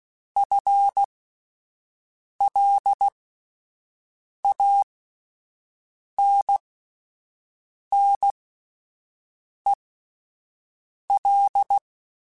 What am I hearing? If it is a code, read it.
Morse code, heard as a keyed tone, about 16 wpm, spaced as for 5 wpm, 800 Hz. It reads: FLANNEL
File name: Spaced / FL Runs sp mp3s / FL Runs sp @10